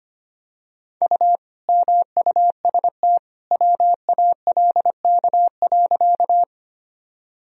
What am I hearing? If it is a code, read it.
Morse code, heard as a keyed tone, about 25 wpm, 700 Hz. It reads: U MUST WALK.
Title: Sentences from Top 300 Words and Abbreviations 25wpm